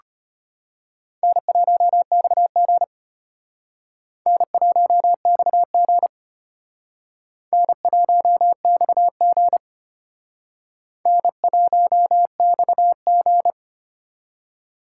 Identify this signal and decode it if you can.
Morse code, heard as a keyed tone, about 32 wpm, 700 Hz. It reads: N1XG N1XG N1XG N1XG